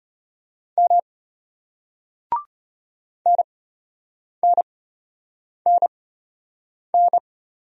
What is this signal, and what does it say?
Morse code, heard as a keyed tone, about 33 wpm, spaced as for 8 wpm, 700 Hz. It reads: M NNNN